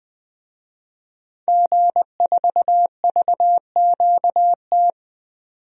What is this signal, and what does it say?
Morse code, heard as a keyed tone, about 20 wpm, 700 Hz. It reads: G4VQT
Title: DXCC-100 Call Signs - 5 Characters - 20wpm